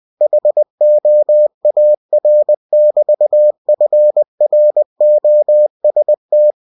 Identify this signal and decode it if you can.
Morse code, heard as a keyed tone, about 20 wpm, 600 Hz. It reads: HOAR <BT> FROST
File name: cw-books / winnie-the-pooh / winnie-20wpm-page-0068